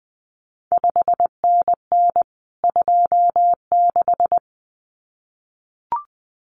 Morse code, heard as a keyed tone, about 20 wpm, 700 Hz. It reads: 5NN 26
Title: CQ WW DX Contest Exchange Encore - 20wpm